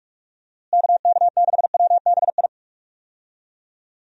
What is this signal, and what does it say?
Morse code, heard as a keyed tone, about 45 wpm, 700 Hz. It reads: KK6WBI
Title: USA Call Signs - 6 Characters - Repeated 3x - 45wpm